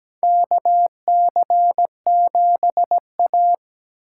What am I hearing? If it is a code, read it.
Morse code, heard as a keyed tone, about 17 wpm, 700 Hz. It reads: KC7A
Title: USA Call Signs - 4 Characters - 8x Character Spacing - 17wpm